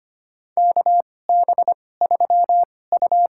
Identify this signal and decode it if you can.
Morse code, heard as a keyed tone, about 25 wpm, spaced as for 18 wpm, 700 Hz. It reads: KB3U
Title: USA Call Signs - 2x Character Spacing - 25wpm